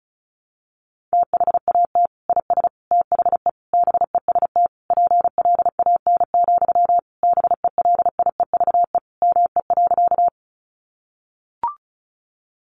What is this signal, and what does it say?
Morse code, heard as a keyed tone, about 34 wpm, 700 Hz. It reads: THAT IS THE BEST PLAN, BELIEVE ME.
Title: Sentences from Top 1000 Words 35wpm